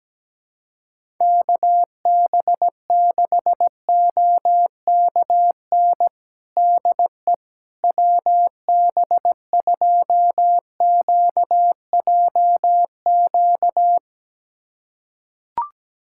Morse code, heard as a keyed tone, about 17 wpm, 700 Hz. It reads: KB6OKN DE WB2QJQ
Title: USA Callsign DE Callsign - 6 Characters - 17wpm